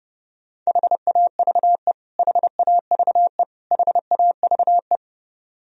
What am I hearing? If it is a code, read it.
Morse code, heard as a keyed tone, about 30 wpm, 700 Hz. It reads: HAVE HAVE HAVE
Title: Top 100 Words Repeated 30wpm